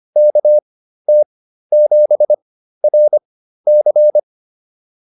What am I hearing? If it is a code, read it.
Morse code, heard as a keyed tone, about 25 wpm, spaced as for 13 wpm, 600 Hz. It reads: KT7RC
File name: POTA207 13